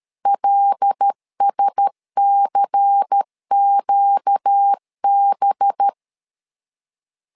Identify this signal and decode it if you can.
Morse code, heard as a keyed tone, about 13 wpm, 800 Hz. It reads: LSCQB
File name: LEZIONI / 5 LBVC / 5 LBVC 12wpm 1955